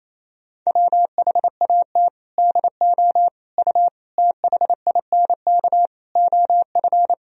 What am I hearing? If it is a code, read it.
Morse code, heard as a keyed tone, about 28 wpm, 700 Hz. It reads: WHAT DO U THINK OF
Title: Sentences from Top 200 Words and Abbreviations 28wpm